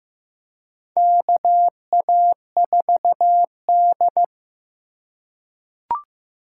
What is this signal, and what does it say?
Morse code, heard as a keyed tone, about 15 wpm, 700 Hz. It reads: KA4D
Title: USA Call Signs - 4 Characters - 3x Character Spacing - 15wpm